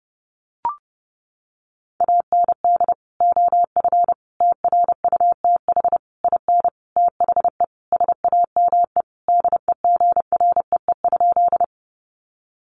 Morse code, heard as a keyed tone, about 28 wpm, 700 Hz. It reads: AND OF TRUTH IN THE SAME DEGREE?
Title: Sentences from Top 1500 Words 30wpm